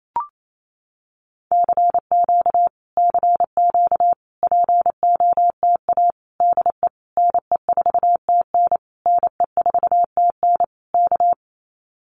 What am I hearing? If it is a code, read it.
Morse code, heard as a keyed tone, about 28 wpm, 700 Hz. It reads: CQ CQ POTA DE NE4TN NE4TN K
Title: Common QSO Phrases Encore 28wpm